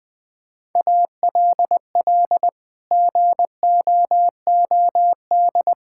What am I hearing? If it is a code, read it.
Morse code, heard as a keyed tone, about 20 wpm, 700 Hz. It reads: ALL GOOD
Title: Sets of 9 Words - 20wpm